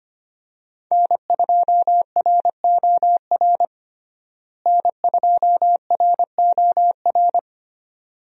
Morse code, heard as a keyed tone, about 25 wpm, 700 Hz. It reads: N2ROR N2ROR
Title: USA Call Signs - 5 Characters - Repeated 2x - 25wpm